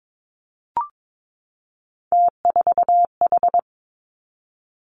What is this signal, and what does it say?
Morse code, heard as a keyed tone, about 22 wpm, 700 Hz. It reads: T4H
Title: Three Challenging Letter-Number Combinations 22wpm